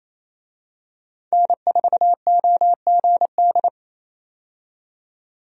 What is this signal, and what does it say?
Morse code, heard as a keyed tone, about 28 wpm, 700 Hz. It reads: N4OGD
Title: USA Call Signs - 5 Characters - 28wpm